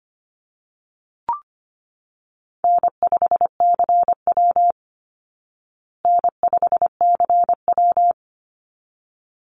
Morse code, heard as a keyed tone, about 25 wpm, 700 Hz. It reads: N5CW N5CW
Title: Call Signs Repeated 3x - 25wpm